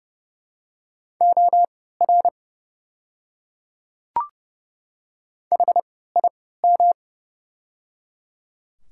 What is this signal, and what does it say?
Morse code, heard as a keyed tone, about 30 wpm, spaced as for 17 wpm, 700 Hz. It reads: OR HIM
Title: 3x Character Spacing - Top 200 Words - 30wpm